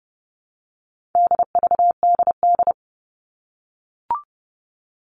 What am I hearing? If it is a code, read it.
Morse code, heard as a keyed tone, about 30 wpm, 700 Hz. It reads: DVDD
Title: All Cut Numbers - 4-Digit 30wpm